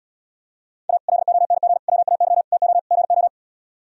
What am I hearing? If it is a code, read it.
Morse code, heard as a keyed tone, about 77 wpm, 700 Hz. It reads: A LONG WAY TO GO